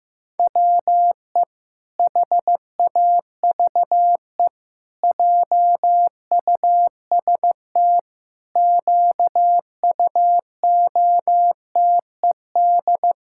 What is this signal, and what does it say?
Morse code, heard as a keyed tone, about 16 wpm, 700 Hz. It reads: WE HAVE JUST QUOTED
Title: Pangram Challenge v2 - 15wpm